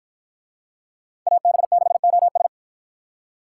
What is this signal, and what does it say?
Morse code, heard as a keyed tone, about 53 wpm, 700 Hz. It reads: AB6OS